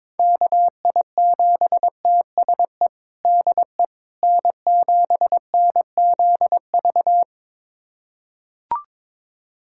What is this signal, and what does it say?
Morse code, heard as a keyed tone, about 22 wpm, 700 Hz. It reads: KI7TSE DE N7NZV
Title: USA Callsign DE Callsign 22wpm